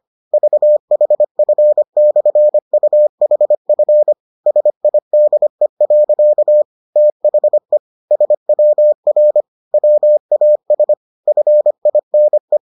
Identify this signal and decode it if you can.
Morse code, heard as a keyed tone, about 25 wpm, 600 Hz. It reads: VHF/UHF SIDE. THE SWR WAS FINE